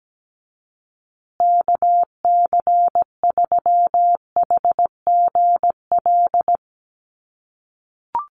Morse code, heard as a keyed tone, about 17 wpm, 700 Hz. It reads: KC3HGL
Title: USA Call Signs Repeated 2x 17wpm